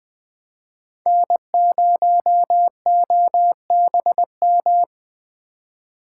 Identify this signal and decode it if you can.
Morse code, heard as a keyed tone, about 20 wpm, 700 Hz. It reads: N0OBM